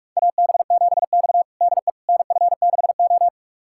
Morse code, heard as a keyed tone, about 44 wpm, 700 Hz. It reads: AB7X DE NF6O
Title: USA Callsign DE Callsign - 4 Characters - 45wpm